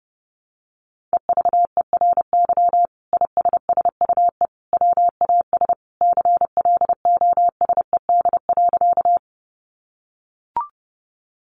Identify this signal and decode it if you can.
Morse code, heard as a keyed tone, about 30 wpm, 700 Hz. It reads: EVERY ISSUE WAS CLOSED.
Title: Sentences from Top 1400 Words 30wpm